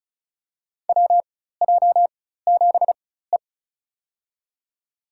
Morse code, heard as a keyed tone, about 36 wpm, spaced as for 16 wpm, 700 Hz. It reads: WJ7E
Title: USA Call Signs - 4 Characters - 4x Character Spacing - 35wpm